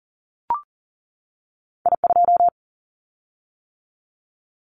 Morse code, heard as a keyed tone, about 38 wpm, 700 Hz. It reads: I2
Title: Callsign Prefixes for Top 50 DXCC - 40wpm